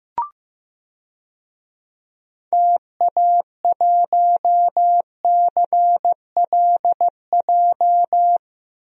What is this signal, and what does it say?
Morse code, heard as a keyed tone, about 15 wpm, 700 Hz. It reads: TA1CLJ
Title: DXCC-100 Call Signs - 6 Characters - 15wpm